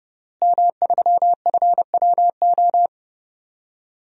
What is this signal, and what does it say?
Morse code, heard as a keyed tone, about 30 wpm, 700 Hz. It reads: M3FWO
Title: DXCC-100 Call Signs - 5 Characters - Repeated 3x - 30wpm